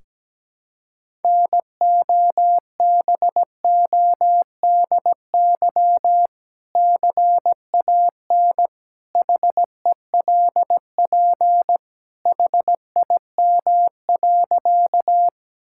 Morse code, heard as a keyed tone, about 17 wpm, 700 Hz. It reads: NOBODY CAN HELP HIM.
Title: Sentences from Top 1600 Words 17wpm